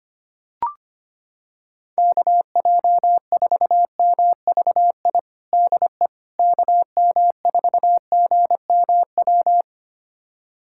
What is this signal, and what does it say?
Morse code, heard as a keyed tone, about 25 wpm, 700 Hz. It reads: KJ4MVI DE KM4GMW
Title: USA Callsign DE Callsign - 6 Characters - 25wpm